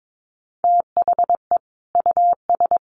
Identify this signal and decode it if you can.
Morse code, heard as a keyed tone, about 22 wpm, 700 Hz. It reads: THE US
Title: Sets of 5 Words Encore2 22wpm